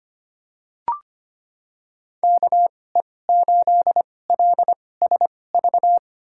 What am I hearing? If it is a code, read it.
Morse code, heard as a keyed tone, about 25 wpm, spaced as for 18 wpm, 700 Hz. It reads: KE8LSV
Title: USA Call Signs - 6 Characters - 2x Character Spacing - 25wpm